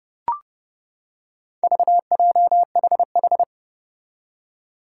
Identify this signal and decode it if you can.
Morse code, heard as a keyed tone, about 30 wpm, 700 Hz. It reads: VJHH
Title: Four Challenging Letter-Number Combinations 30wpm